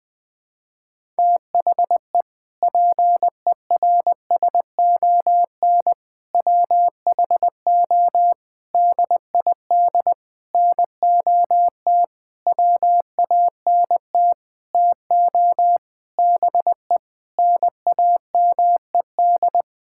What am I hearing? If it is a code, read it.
Morse code, heard as a keyed tone, about 20 wpm, 700 Hz. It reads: THE PERSON WHO DID NOT WANT TO BE NAMED
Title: Sentences with 7-Word Phrases - Spoken Beginning - 20wpm